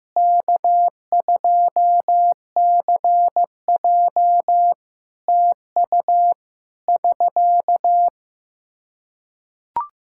Morse code, heard as a keyed tone, about 15 wpm, 700 Hz. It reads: K2CJ TU <SK>
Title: K1USN SST QSO Practice 15wpm